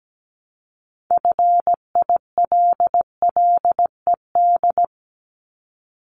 Morse code, heard as a keyed tone, about 17 wpm, 690 Hz. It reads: FILLED
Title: Multi-letter Phonemes Repeated 3x 17wpm